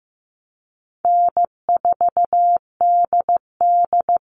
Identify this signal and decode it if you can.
Morse code, heard as a keyed tone, about 15 wpm, 700 Hz. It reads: N4DD
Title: Easy Call Signs Repeated 2x - 15wpm